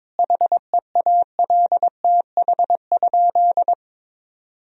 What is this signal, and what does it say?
Morse code, heard as a keyed tone, about 22 wpm, 700 Hz. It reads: HEALTH?
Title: Sentences from Top 1000 Words - Review Part 3 - 22wpm